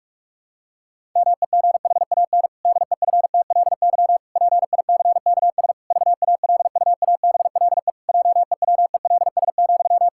Morse code, heard as a keyed tone, about 41 wpm, 700 Hz. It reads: MEGHAN DEFTLY PICKS VALUABLE JEWELS,
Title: Pangram Challenge v2 - 45wpm